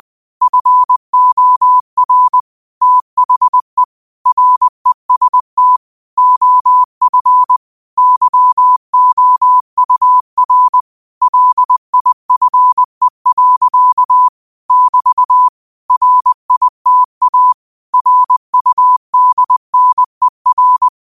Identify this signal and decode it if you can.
Morse code, heard as a keyed tone, about 20 wpm, 1000 Hz. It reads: FOR THE REST OF YOUR LIFE. <BT> RITA RUDNER
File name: qotdfiles / 20wpm